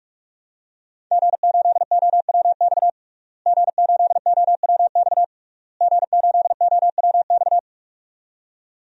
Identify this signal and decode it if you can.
Morse code, heard as a keyed tone, about 45 wpm, 700 Hz. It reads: G8OWX G8OWX G8OWX
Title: DXCC-100 Call Signs - 5 Characters - Repeated 3x - 45wpm